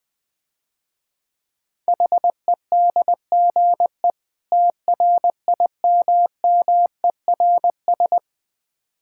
Morse code, heard as a keyed tone, about 20 wpm, 700 Hz. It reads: HEDGE TRIMMERS
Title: Difficult Words - Speed-Racing with Hints - 20wpm